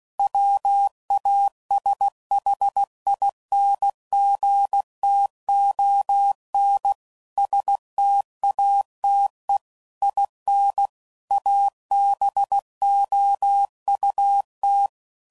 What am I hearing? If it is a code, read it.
Morse code, heard as a keyed tone, about 16 wpm, 800 Hz. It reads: WASHINGTON STATE IN ABOUT
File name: CW Code Groups @15WPM / QSOs / WD0BGP Spaced